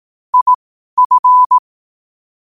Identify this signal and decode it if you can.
Morse code, heard as a keyed tone, about 18 wpm, spaced as for 12 wpm, 1000 Hz. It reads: IF
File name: qotdfiles / 12wpm